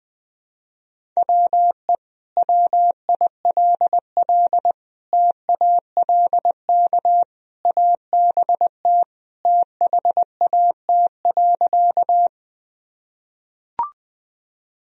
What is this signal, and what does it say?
Morse code, heard as a keyed tone, about 20 wpm, 700 Hz. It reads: WE WILL TALK ABT THAT.
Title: Sentences from Top 400 Words and Abbreviations 20wpm